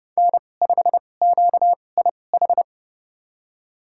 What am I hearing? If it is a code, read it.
Morse code, heard as a keyed tone, about 30 wpm, spaced as for 22 wpm, 700 Hz. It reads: N5QIH
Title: USA Call Signs - 2x Character Spacing - 30wpm